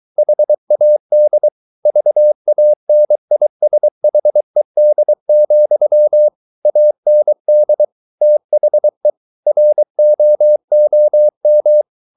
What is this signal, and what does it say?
Morse code, heard as a keyed tone, about 23 wpm, 600 Hz. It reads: HAD VANISHED, AND THE ROOM